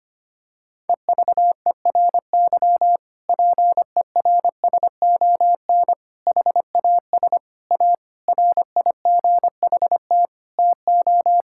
Morse code, heard as a keyed tone, about 25 wpm, 700 Hz. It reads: EVERY PERSON HAS A RIGHT TO
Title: Sentences from Top 1000 Words - Review Part 5 - 25wpm